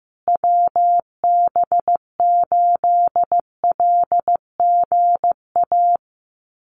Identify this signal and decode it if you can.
Morse code, heard as a keyed tone, about 15 wpm, 700 Hz. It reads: WB8LGA
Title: USA Call Signs Repeated 2x 15wpm